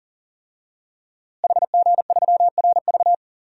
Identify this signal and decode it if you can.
Morse code, heard as a keyed tone, about 40 wpm, 700 Hz. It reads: HG3RV
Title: DXCC-100 Call Signs - 5 Characters - Repeated 2x - 40wpm